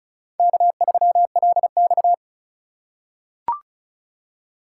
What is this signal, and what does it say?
Morse code, heard as a keyed tone, about 34 wpm, 700 Hz. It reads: K3LX
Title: USA Call Signs - 4 Characters - 3x Character Spacing - 35wpm